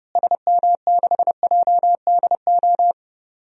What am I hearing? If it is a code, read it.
Morse code, heard as a keyed tone, about 30 wpm, 700 Hz. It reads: SM6JDO
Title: DXCC-100 Call Signs - 6 Characters - Repeated 3x - 30wpm